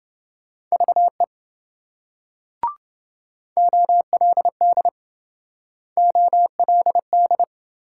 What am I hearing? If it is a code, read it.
Morse code, heard as a keyed tone, about 27 wpm, 700 Hz. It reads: VE OLD OLD